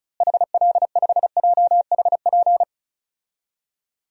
Morse code, heard as a keyed tone, about 36 wpm, 700 Hz. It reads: HL5JHP
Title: DXCC-100 Call Signs - 6 Characters - Repeated 2x - 35wpm